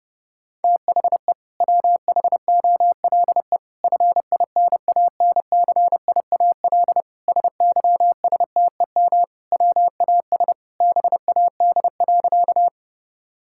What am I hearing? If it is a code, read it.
Morse code, heard as a keyed tone, about 30 wpm, 700 Hz. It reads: THE WHOLE FINANCIAL SYSTEM WAS BAD.
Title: Sentences from Top 1000 Words - Review Part 5 - 30wpm